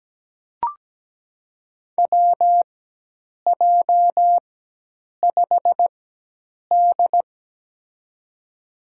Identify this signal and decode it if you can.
Morse code, heard as a keyed tone, about 17 wpm, spaced as for 8 wpm, 700 Hz. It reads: WJ5D
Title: USA Call Signs - 4 Characters - 4x Character Spacing - 17wpm